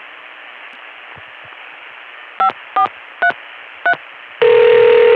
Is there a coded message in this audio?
Yes: DTMF from 2.4 to 3.9 s, heard as telephone touch tones, tones of about 100 ms. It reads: 5433